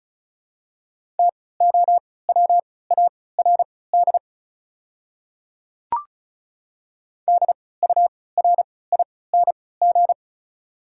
Morse code, heard as a keyed tone, about 36 wpm, spaced as for 20 wpm, 700 Hz. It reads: TOWARD DURING
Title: 3x Character Spacing - Top 500 Words - 35wpm